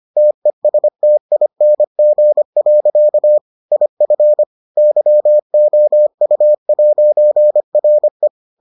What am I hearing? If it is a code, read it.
Morse code, heard as a keyed tone, about 25 wpm, 600 Hz. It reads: TESTING. IF YOU'RE